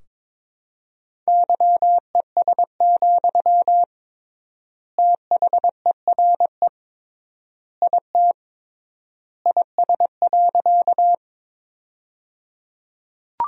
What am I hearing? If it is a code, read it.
Morse code, heard as a keyed tone, about 22 wpm, 700 Hz. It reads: YES, THERE IT IS.
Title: 3x Word-Spacing - Sentences from Top 500 Words - 22wpm